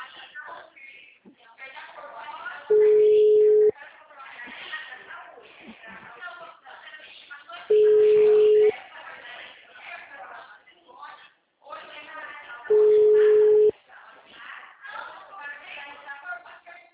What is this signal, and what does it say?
Ringback tone (Europe, 425 Hz), with speech.